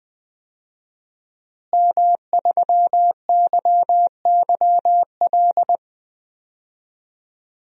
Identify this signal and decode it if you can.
Morse code, heard as a keyed tone, about 20 wpm, 700 Hz. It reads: M3YYL
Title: DXCC-100 Call Signs - 5 Characters - Repeated 2x - 20wpm